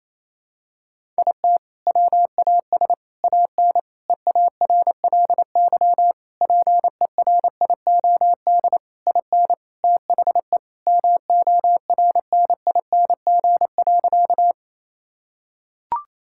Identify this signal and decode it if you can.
Morse code, heard as a keyed tone, about 28 wpm, 700 Hz. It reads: IT WAS AN EARLY PERIOD IN THE MORNING.
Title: Sentences from Top 1000 Words - Review Part 5 - 28wpm